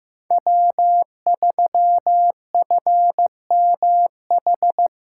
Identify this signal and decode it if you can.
Morse code, heard as a keyed tone, about 18 wpm, 700 Hz. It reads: W3FMH
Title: Speed Racing - USA 1x3 Call Signs 15wpm